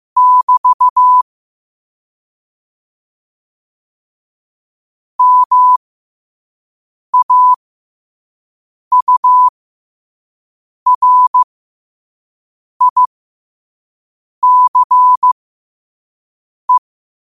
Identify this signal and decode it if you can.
Morse code, heard as a keyed tone, about 15 wpm, spaced as for 5 wpm, 1000 Hz. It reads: <BT> MAURICE
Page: qotd-at-5-wpm